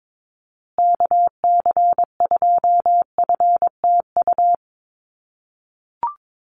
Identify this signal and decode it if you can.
Morse code, heard as a keyed tone, about 22 wpm, 700 Hz. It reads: KC2FTU
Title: USA Call Signs Repeated 2x 22wpm